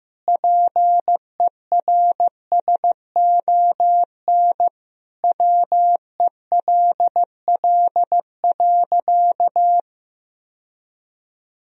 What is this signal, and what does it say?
Morse code, heard as a keyed tone, about 15 wpm, 700 Hz. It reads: PERSON WELL.
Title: Sentences from Top 1000 Words - Review Part 3 - 15wpm